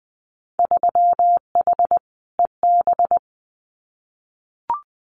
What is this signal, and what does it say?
Morse code, heard as a keyed tone, about 20 wpm, 700 Hz. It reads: 3H EB
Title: Winter Field Day Exchanges Encore 20wpm